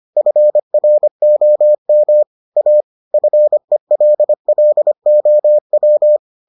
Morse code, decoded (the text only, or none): FROM A FELLOW